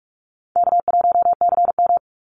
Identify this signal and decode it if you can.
Morse code, heard as a keyed tone, about 44 wpm, 700 Hz. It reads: K1CM
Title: NA SOTA Chaser Callsigns 45wpm